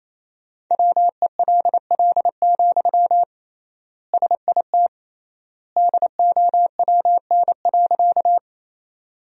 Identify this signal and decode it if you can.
Morse code, heard as a keyed tone, about 28 wpm, 700 Hz. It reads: WELL, SIT DOWN.
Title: 3x Word-Spacing - Sentences from Top 500 Words - 28wpm